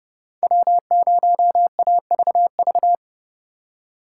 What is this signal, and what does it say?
Morse code, heard as a keyed tone, about 33 wpm, 700 Hz. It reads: W0AVV